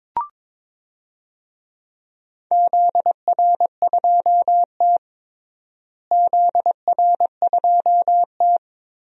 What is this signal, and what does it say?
Morse code, heard as a keyed tone, about 22 wpm, 700 Hz. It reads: ZR2T ZR2T
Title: DXCC-100 Call Signs - 4 Characters - Repeated 3x - 22wpm